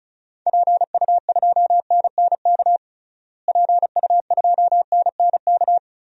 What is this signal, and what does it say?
Morse code, heard as a keyed tone, about 36 wpm, 700 Hz. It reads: PU2NNK PU2NNK
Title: DXCC-100 Call Signs - 6 Characters - Repeated 2x - 35wpm